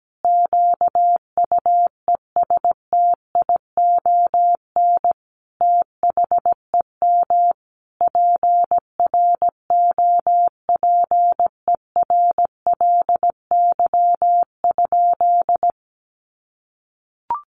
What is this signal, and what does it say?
Morse code, heard as a keyed tone, about 17 wpm, 700 Hz. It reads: QUESTION THEM PROPERLY?
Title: Sentences from Top 1100 Words 17wpm